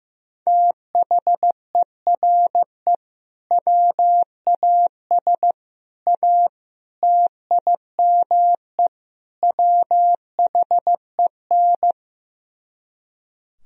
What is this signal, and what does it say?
Morse code, heard as a keyed tone, about 15 wpm, 690 Hz. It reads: THERE WAS A TIME WHEN
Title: Speed-Racing - Sets of 5 Words 15wpm